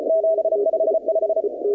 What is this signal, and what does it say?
Morse code, heard as a keyed tone, about 33 wpm, 660 Hz, with noise. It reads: 755
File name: contests / 2019-cw-wpx-kt5j-k5pi- -2330 0-21